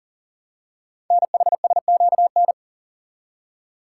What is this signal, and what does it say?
Morse code, heard as a keyed tone, about 40 wpm, 700 Hz. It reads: NHSQN
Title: Code Groups - 5 Letter:Numbers - 40wpm